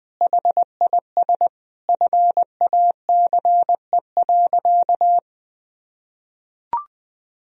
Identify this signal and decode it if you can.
Morse code, heard as a keyed tone, about 20 wpm, 700 Hz. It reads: HIS FACE.
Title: Sentences from Top 1000 Words - Review Part 2 - 20wpm